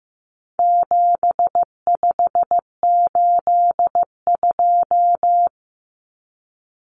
Morse code, heard as a keyed tone, about 15 wpm, 700 Hz. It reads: 7582
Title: Numbers - 4-Digit 15wpm